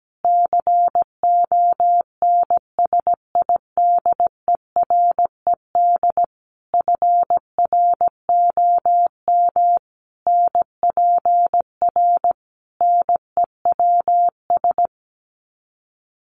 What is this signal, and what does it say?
Morse code, heard as a keyed tone, about 17 wpm, 700 Hz. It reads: CONSIDERED FROM NPR NEWS
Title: Sets of 5 Words Encore 17wpm